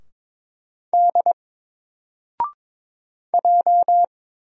Morse code, heard as a keyed tone, about 22 wpm, 700 Hz. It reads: D J
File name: Single-Letter-Number-22wpm